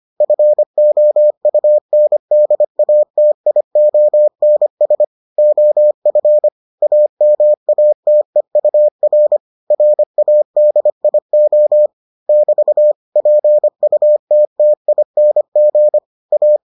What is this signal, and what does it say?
Morse code, heard as a keyed tone, about 25 wpm, 600 Hz. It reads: FOUNDATIONS OF AMATEUR RADIO <BT> PUTTING A